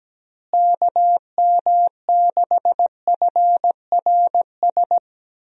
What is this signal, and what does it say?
Morse code, heard as a keyed tone, about 17 wpm, 700 Hz. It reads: KM6FRS
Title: USA Call Signs - 6 Characters - Repeated 3x - 17wpm